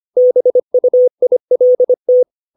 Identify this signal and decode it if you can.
Morse code, heard as a keyed tone, about 25 wpm, 500 Hz. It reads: BUILT
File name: 632 words / 632 Words 25WPM